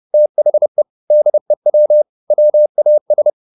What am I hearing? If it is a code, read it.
Morse code, heard as a keyed tone, about 30 wpm, 600 Hz. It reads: THE DEW WAS
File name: the old man and the sea / the old man and the sea 1 30wpm